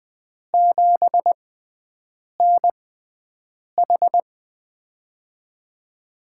Morse code, heard as a keyed tone, about 20 wpm, spaced as for 7 wpm, 700 Hz. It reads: 7NH
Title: USA Call Signs - 6x Character Spacing - 20wpm